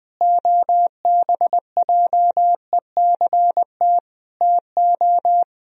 Morse code, heard as a keyed tone, about 21 wpm, 700 Hz. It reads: OBJECT TO
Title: Sentences from Top 1000 Words - Review Part 3 - 20wpm